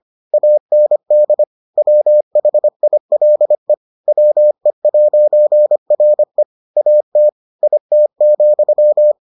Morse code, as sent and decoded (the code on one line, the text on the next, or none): .- -. -.. / .-- .... .. .-.. / .-- . .----. .-. / .- - / .. - --..--
AND WHILE WE'RE AT IT,